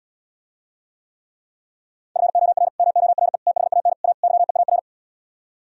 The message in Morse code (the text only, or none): FOR MORE THAN A YEAR